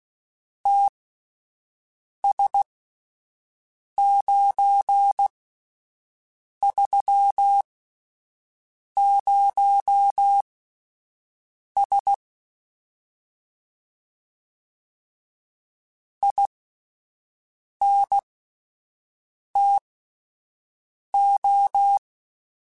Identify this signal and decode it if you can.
Morse code, heard as a keyed tone, about 16 wpm, spaced as for 5 wpm, 800 Hz. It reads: TS930S INTO